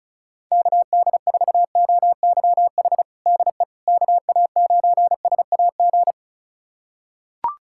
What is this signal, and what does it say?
Morse code, heard as a keyed tone, about 34 wpm, 700 Hz. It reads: KD4OYH DE KA9SAG
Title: USA Callsign DE Callsign - 6 Characters - 35wpm